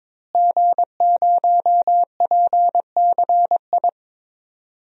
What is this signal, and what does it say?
Morse code, heard as a keyed tone, about 22 wpm, 700 Hz. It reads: G0PCI
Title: DXCC-100 Call Signs - 5 Characters - Repeated 3x - 22wpm